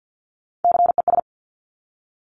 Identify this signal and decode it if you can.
Morse code, heard as a keyed tone, about 50 wpm, 700 Hz. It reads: CES
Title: Top Three Letter Combinations in Words 50wpm